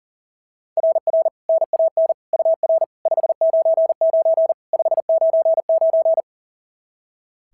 Morse code, heard as a keyed tone, about 37 wpm, 690 Hz. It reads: RR NAN UR 599 599
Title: SOTA QSO Practice 40wpm